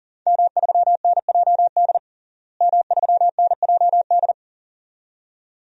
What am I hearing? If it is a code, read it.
Morse code, heard as a keyed tone, about 40 wpm, 700 Hz. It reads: M3NJD M3NJD
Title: DXCC-100 Call Signs - 5 Characters - Repeated 2x - 40wpm